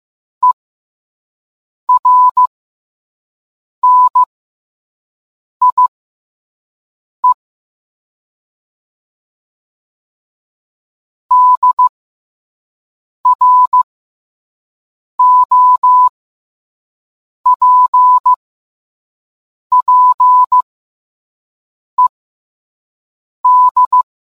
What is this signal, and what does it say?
Morse code, heard as a keyed tone, about 15 wpm, spaced as for 5 wpm, 1000 Hz. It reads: ERNIE DROPPED